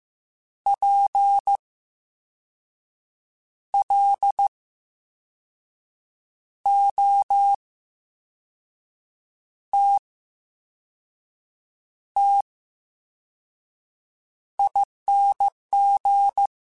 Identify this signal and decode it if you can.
Morse code, heard as a keyed tone, about 15 wpm, spaced as for 5 wpm, 800 Hz. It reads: PLOTTING
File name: Special INGs-mp3s / Special INGs-mp3s / VerySlowToFaster Ing @13